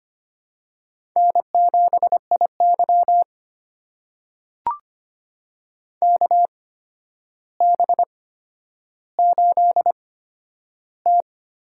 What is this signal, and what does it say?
Morse code, heard as a keyed tone, about 25 wpm, spaced as for 11 wpm, 700 Hz. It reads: N7IY KB8T
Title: USA Call Signs - 4 Characters - 8x Character Spacing - 25wpm